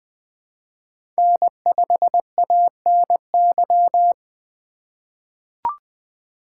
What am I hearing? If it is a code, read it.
Morse code, heard as a keyed tone, about 20 wpm, 700 Hz. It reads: N5ANY
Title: USA Call Signs - 5 Characters - 4x Character Spacing - 20wpm